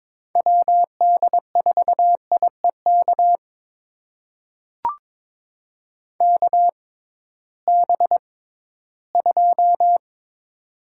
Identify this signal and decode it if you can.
Morse code, heard as a keyed tone, about 22 wpm, spaced as for 14 wpm, 700 Hz. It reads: WD4IEK KB2
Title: USA Call Signs - 6 Characters - 6x Character Spacing - 22wpm